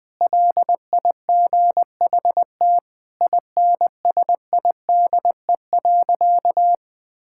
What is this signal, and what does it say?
Morse code, heard as a keyed tone, about 20 wpm, 700 Hz. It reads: LIGHT INSIDE.
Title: Sentences from Top 1000 Words - Review Part 6 - 20wpm